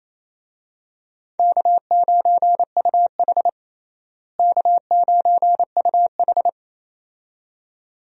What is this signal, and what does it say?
Morse code, heard as a keyed tone, about 28 wpm, 700 Hz. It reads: K9UH K9UH
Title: USA Call Signs - 4 Characters - Repeated 2x - 28wpm